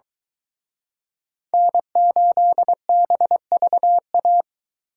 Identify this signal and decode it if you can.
Morse code, heard as a keyed tone, about 23 wpm, 700 Hz. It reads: N8BVA